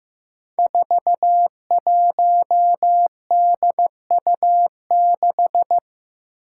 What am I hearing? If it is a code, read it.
Morse code, heard as a keyed tone, about 15 wpm, 700 Hz. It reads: 41DU6
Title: Five Challenging Letter-Number Combinations 15wpm